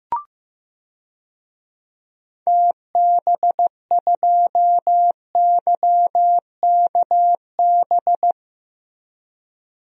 Morse code, heard as a keyed tone, about 15 wpm, 700 Hz. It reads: TB2YKB
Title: DXCC-100 Call Signs - 6 Characters - Repeated 2x - 15wpm